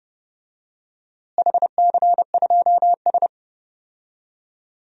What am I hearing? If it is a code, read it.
Morse code, heard as a keyed tone, about 30 wpm, 700 Hz. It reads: HC2S
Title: DXCC-100 Call Signs - 4 Characters - Repeated 3x - 30wpm